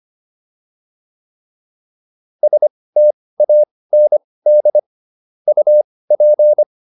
Morse code, heard as a keyed tone, about 25 wpm, spaced as for 18 wpm, 600 Hz. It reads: STAND UP